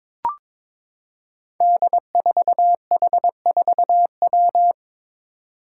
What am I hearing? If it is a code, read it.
Morse code, heard as a keyed tone, about 22 wpm, 700 Hz. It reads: D4H4W
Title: Five Challenging Letter-Number Combinations 22wpm